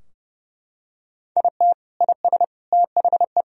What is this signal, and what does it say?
Morse code, heard as a keyed tone, about 32 wpm, 700 Hz. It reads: IT IS THE
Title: Sentences from Top 1000 Words - Review Part 4 - 30wpm